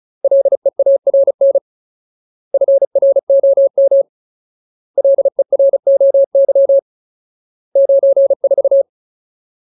Morse code, heard as a keyed tone, about 34 wpm, 550 Hz. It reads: LEARN FROM LEROY 94
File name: Ultimate Word List / 35WPM Ultimate